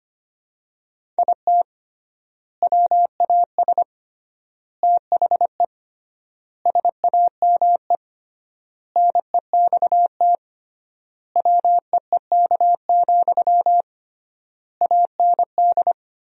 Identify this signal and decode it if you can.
Morse code, heard as a keyed tone, about 25 wpm, 700 Hz. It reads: IT WAS THE SAME NEXT WEEK, AND
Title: 3x Word-Spacing - Sentences from Top 500 Words - 25wpm